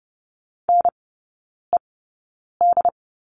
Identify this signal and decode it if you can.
Morse code, heard as a keyed tone, about 30 wpm, spaced as for 9 wpm, 700 Hz. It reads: NED
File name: 081 Lesson 14e - Number 2 - Three Characters 30wpm